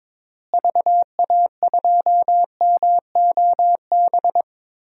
Morse code, heard as a keyed tone, about 21 wpm, 700 Hz. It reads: VA2MOB
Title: DXCC-100 Call Signs - 6 Characters - Repeated 3x - 22wpm